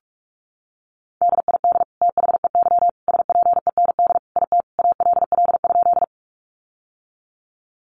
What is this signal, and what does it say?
Morse code, heard as a keyed tone, about 44 wpm, 700 Hz. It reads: DID THEY SPEND IT ALL?
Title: Sentences from Top 800 Words 45wpm